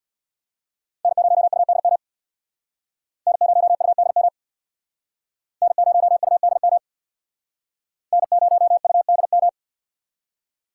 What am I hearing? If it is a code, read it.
Morse code, heard as a keyed tone, about 61 wpm, 700 Hz. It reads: N0UDM N0UDM N0UDM N0UDM
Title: Speed Racing - USA 1x3 Call Signs 50wpm